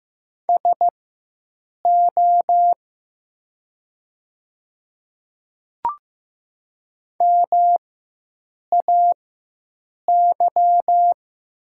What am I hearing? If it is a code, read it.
Morse code, heard as a keyed tone, about 15 wpm, spaced as for 7 wpm, 700 Hz. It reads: SO MAY
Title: 4x Character Spacing - Top 100 Words - 15wpm